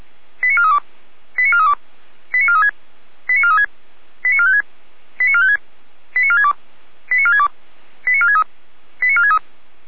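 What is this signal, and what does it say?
Five-tone selective call (ZVEI), heard as quick runs of five tones, tones of about 70 ms. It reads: 89432 89432 89436 89436 89456 89456 89462 89462 89463 89463
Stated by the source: ZVEI five-tone call received by SDR, feuerwehr-orlamuende.org fire brigade.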